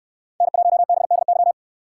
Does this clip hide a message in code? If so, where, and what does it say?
Yes: Morse code from 0.4 to 1.5 s, heard as a keyed tone, about 68 wpm, 700 Hz. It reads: N1BDY